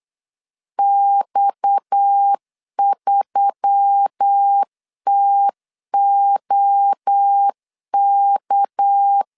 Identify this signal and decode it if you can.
Morse code, heard as a keyed tone, about 8 wpm, 800 Hz. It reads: X3TOK
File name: LEZIONI / 6 PKX0 / 6 PKX 8wpm 3041